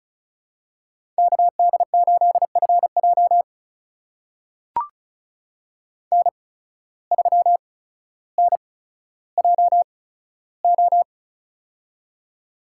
Morse code, heard as a keyed tone, about 36 wpm, spaced as for 15 wpm, 700 Hz. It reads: KD8FJ N3NJO
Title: USA Call Signs - 5 Characters - 8x Character Spacing - 35wpm